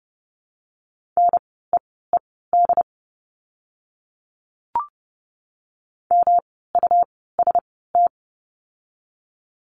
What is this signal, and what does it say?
Morse code, heard as a keyed tone, about 30 wpm, spaced as for 17 wpm, 700 Hz. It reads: NEED MUST
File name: course / 30:10wpm / 237 Lesson 38f - Letter q - Words 30wpm